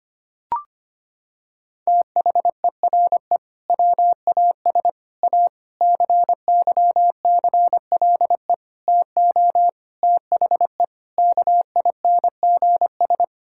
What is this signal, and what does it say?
Morse code, heard as a keyed tone, about 25 wpm, 700 Hz. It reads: THERE WAS A CYCLE TO THE KINGS
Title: Sentences from Top 1000 Words - Review Part 6 - 25wpm